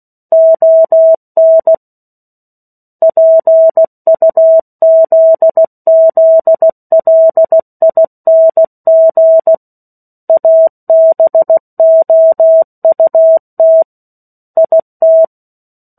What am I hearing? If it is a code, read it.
Morse code, heard as a keyed tone, about 16 wpm, 650 Hz. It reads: ON PUZZLING ABOUT IT
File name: 03-Chapter-03 16 WPM